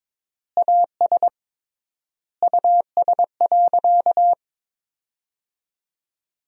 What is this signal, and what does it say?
Morse code, heard as a keyed tone, about 22 wpm, 700 Hz. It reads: AS US.
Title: 3x Word-Spacing - Sentences from Top 500 Words - 22wpm